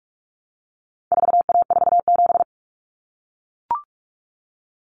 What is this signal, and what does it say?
Morse code, heard as a keyed tone, about 44 wpm, 700 Hz. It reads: 4A47
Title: Common Cut Numbers - 4-Digit 45wpm